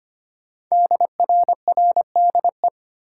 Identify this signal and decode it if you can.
Morse code, heard as a keyed tone, about 25 wpm, 700 Hz. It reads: DRRDE
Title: Code Groups - 5 Letters - 25wpm